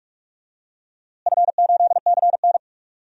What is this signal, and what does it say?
Morse code, heard as a keyed tone, about 42 wpm, 690 Hz. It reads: F8CN
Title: DXCC-100 Call Signs - 4 Characters - Repeated 3x - 45wpm